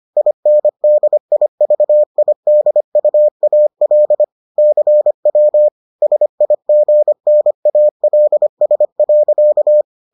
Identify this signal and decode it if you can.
Morse code, heard as a keyed tone, about 25 wpm, 600 Hz. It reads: INDIVIDUAL CW SIGNALS.